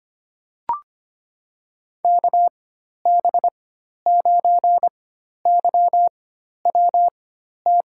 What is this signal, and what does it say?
Morse code, heard as a keyed tone, about 25 wpm, spaced as for 12 wpm, 700 Hz. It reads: KB9YWT
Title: USA Call Signs - 6 Characters - 4x Character Spacing - 25wpm